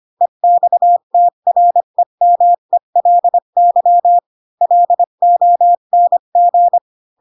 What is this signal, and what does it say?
Morse code, heard as a keyed tone, about 25 wpm, 700 Hz. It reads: EXTREMELY LONG
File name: Pinocchio / Pinocchio ch3 20 25 ik1hge